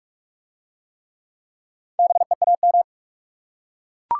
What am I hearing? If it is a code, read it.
Morse code, heard as a keyed tone, about 43 wpm, 700 Hz. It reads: BEAM